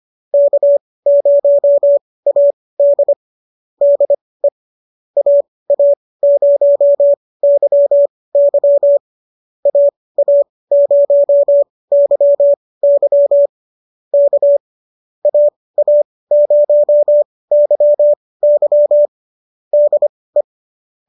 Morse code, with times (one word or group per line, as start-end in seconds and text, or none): 0.3-3.1 K0AD
3.8-4.5 DE
5.2-9.0 AA0YY
9.7-13.5 AA0YY
14.1-14.6 K
15.3-19.1 AA0YY
19.7-20.4 DE